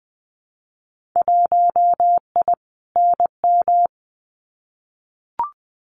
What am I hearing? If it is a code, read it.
Morse code, heard as a keyed tone, about 20 wpm, 700 Hz. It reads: 1I NM
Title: Winter Field Day Exchanges Encore 20wpm